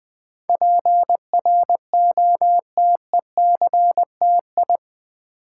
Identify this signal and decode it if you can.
Morse code, heard as a keyed tone, about 20 wpm, 700 Hz. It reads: PROTECTI
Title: Build A Word - 4 Letters At A Time - 10-Letter Words - 20wpm